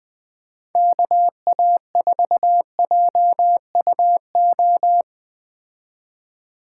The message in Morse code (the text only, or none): KA4JUO